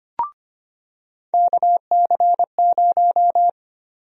Morse code, heard as a keyed tone, about 25 wpm, 700 Hz. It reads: KC0
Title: USA Call Sign Prefixes - 25wpm